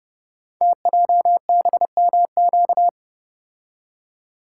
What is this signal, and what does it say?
Morse code, heard as a keyed tone, about 30 wpm, 700 Hz. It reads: TJBMQ
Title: Code Groups - 5 Letters - 30wpm